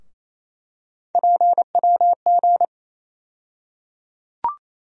Morse code, heard as a keyed tone, about 28 wpm, 700 Hz. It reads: PWG
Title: USA Call Sign Postfixes - 3 Letters 28wpm